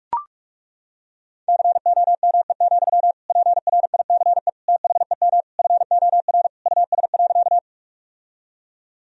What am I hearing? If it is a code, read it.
Morse code, heard as a keyed tone, about 41 wpm, 700 Hz. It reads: COME, PRICE THEM FOR US.
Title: Sentences from Top 1000 Words - Review Part 4 - 45wpm